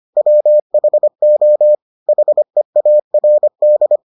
Morse code, heard as a keyed tone, about 25 wpm, 600 Hz. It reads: WHO HEARD